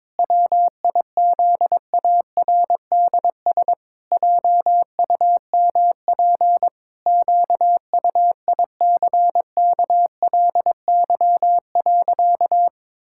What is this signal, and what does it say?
Morse code, heard as a keyed tone, about 22 wpm, 700 Hz. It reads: WIZARDS JUMP QUICKLY.
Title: Pangram Challenge v2 - 22wpm